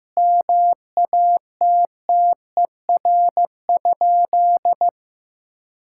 Morse code, heard as a keyed tone, about 14 wpm, 700 Hz. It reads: MATTER?
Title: Sentences from Top 1000 Words - Review Part 6 - 15wpm